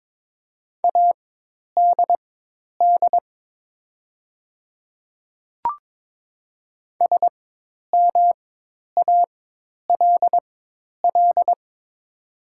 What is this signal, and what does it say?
Morse code, heard as a keyed tone, about 22 wpm, spaced as for 10 wpm, 700 Hz. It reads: ADD SMALL
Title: 4x Character Spacing - Top 200 Words - 22wpm